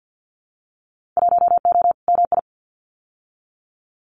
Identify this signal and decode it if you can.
Morse code, heard as a keyed tone, about 50 wpm, 700 Hz. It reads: 1O MI
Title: Winter Field Day Exchanges Encore 50wpm